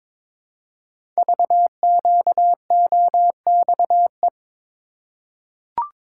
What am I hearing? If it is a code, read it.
Morse code, heard as a keyed tone, about 22 wpm, 700 Hz. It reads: VQOXE
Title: Code Groups - 5 Letters - 22wpm